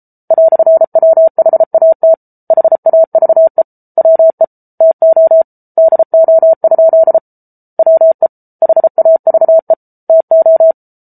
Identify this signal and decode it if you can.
Morse code, heard as a keyed tone, about 34 wpm, 650 Hz. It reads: "WHAT HAVE WE TO DO? WE HAVE TO